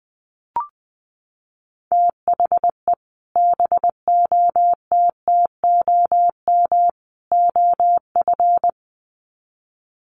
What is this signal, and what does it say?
Morse code, heard as a keyed tone, about 20 wpm, 700 Hz. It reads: THE BOTTOM OF
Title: Sets of 3 Words 20wpm